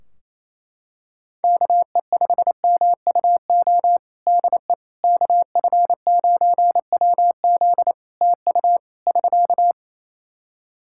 Morse code, heard as a keyed tone, about 28 wpm, 700 Hz. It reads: KE5MUO DE KF9WZ TU <SK>
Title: K1USN SST QSO Practice 28wpm